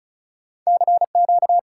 Morse code, heard as a keyed tone, about 48 wpm, spaced as for 41 wpm, 700 Hz. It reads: CQ
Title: Speed-Racing Common QSO Phrases - 35wpm